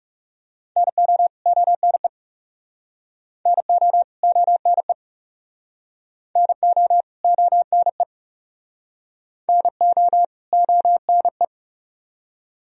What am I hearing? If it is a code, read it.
Morse code, heard as a keyed tone, about 40 wpm, 700 Hz. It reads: NO ONE NO ONE NO ONE NO ONE